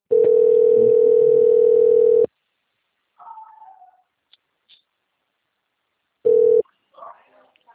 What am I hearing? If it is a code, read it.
Ringback tone (North America), 440 + 480 Hz.
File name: out- - - - - .26207